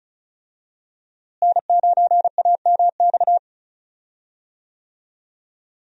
Morse code, heard as a keyed tone, about 36 wpm, 700 Hz. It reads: N9AMX
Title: USA Call Signs - 5 Characters - Repeated 3x - 35wpm